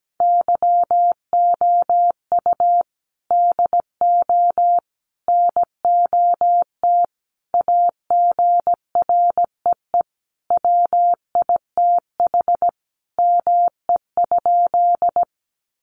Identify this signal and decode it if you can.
Morse code, heard as a keyed tone, about 17 wpm, 700 Hz. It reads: YOU DO NOT AGREE WITH ME?
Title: Sentences from Top 1100 Words 17wpm